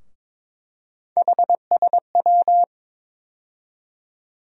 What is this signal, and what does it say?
Morse code, heard as a keyed tone, about 22 wpm, 700 Hz. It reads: HSW
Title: USA Call Sign Postfixes - 3 Letters 22wpm